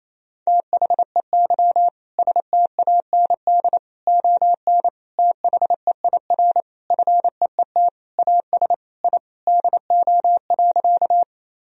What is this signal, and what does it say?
Morse code, heard as a keyed tone, about 27 wpm, 700 Hz. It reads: THEY STAND ON THEIR FEET AS I DO.
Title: Sentences from Top 1000 Words - Review Part 1 - 28wpm